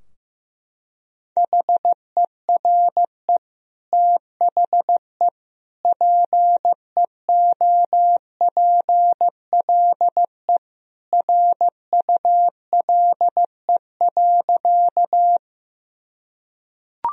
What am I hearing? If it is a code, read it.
Morse code, heard as a keyed tone, about 15 wpm, 700 Hz. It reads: HERE THE PEOPLE RULE.
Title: Sentences from Top 1000 Words - Review Part 2 - 15wpm